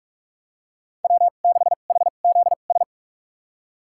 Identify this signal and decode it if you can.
Morse code, heard as a keyed tone, about 46 wpm, spaced as for 32 wpm, 700 Hz. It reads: W6HZS
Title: USA Call Signs - 2x Character Spacing - 45wpm